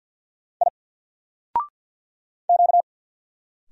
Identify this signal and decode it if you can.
Morse code, heard as a keyed tone, about 50 wpm, 700 Hz. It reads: I Q
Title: Single Character (Letter:Number:Punctuation:Prosign) Alt 50wpm